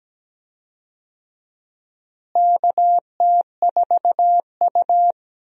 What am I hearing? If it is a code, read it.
Morse code, heard as a keyed tone, about 17 wpm, 700 Hz. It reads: KT4U
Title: USA Call Signs - 4 Characters - Repeated 3x - 17wpm